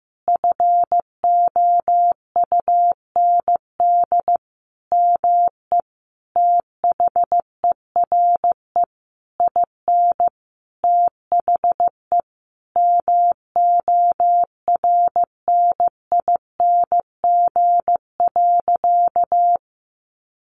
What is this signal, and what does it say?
Morse code, heard as a keyed tone, about 15 wpm, 700 Hz. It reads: FOUND ME THERE IN THE MORNING.
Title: Sentences from Top 1200 Words 15wpm